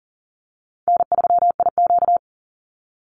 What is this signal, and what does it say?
Morse code, heard as a keyed tone, about 40 wpm, 700 Hz. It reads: N3IQ
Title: Easy Call Signs Repeated 2x - 40wpm